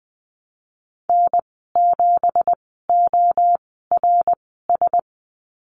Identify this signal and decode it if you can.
Morse code, heard as a keyed tone, about 20 wpm, spaced as for 14 wpm, 700 Hz. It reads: N7ORS
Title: 186 Lesson 30g - Letter b - Call Signs 20wpm